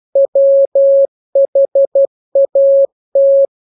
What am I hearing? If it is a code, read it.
Morse code, heard as a keyed tone, about 12 wpm, 550 Hz. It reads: WHAT
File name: jokes / joke 16 12 0000